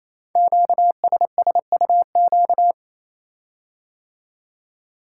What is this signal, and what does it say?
Morse code, heard as a keyed tone, about 29 wpm, 700 Hz. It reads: QSSUQ
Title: Code Groups - 5 Letters - 28wpm